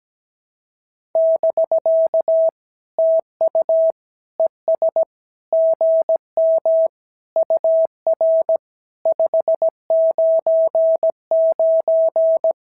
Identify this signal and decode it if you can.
Morse code, heard as a keyed tone, about 17 wpm, 650 Hz. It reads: <BK> TU ES GM UR 599
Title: SOTA QSO Practice 17wpm